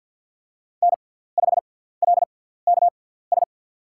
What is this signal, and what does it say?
Morse code, heard as a keyed tone, about 52 wpm, spaced as for 17 wpm, 700 Hz. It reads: N5LKS
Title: USA Call Signs - 5 Characters - 6x Character Spacing - 50wpm